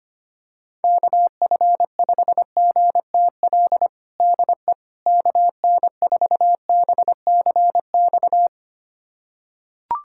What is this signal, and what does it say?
Morse code, heard as a keyed tone, about 25 wpm, 700 Hz. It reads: KF5GTL DE KN4BCX
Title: USA Callsign DE Callsign 25wpm